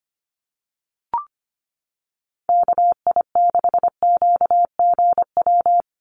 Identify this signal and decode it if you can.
Morse code, heard as a keyed tone, about 25 wpm, 700 Hz. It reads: KI6QGW
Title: USA Call Signs 25wpm